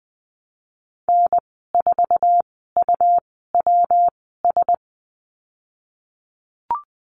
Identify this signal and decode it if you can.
Morse code, heard as a keyed tone, about 20 wpm, spaced as for 15 wpm, 700 Hz. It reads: N4UWS
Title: 128 Lesson 21g - Number 6 - Call Signs 20wpm